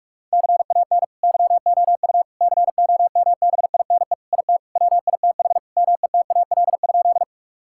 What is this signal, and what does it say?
Morse code, heard as a keyed tone, about 42 wpm, 700 Hz. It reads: CAN YOU COMBINE IT WITH METAL?
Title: Sentences from Top 1600 Words 45wpm